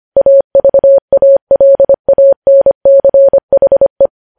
Morse code, heard as a keyed tone, about 25 wpm, 550 Hz. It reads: AVALANCHE
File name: podcast / News headlines-1x-25x25